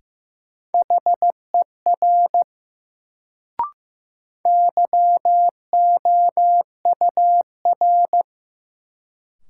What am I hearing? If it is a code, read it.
Morse code, heard as a keyed tone, about 15 wpm, 700 Hz. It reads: HER YOUR
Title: Sentences from Top 100 Words - New Word Drill - 15wpm